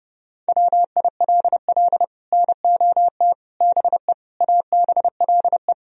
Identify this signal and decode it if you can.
Morse code, heard as a keyed tone, about 29 wpm, 700 Hz. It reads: WILL NOT BE ABLE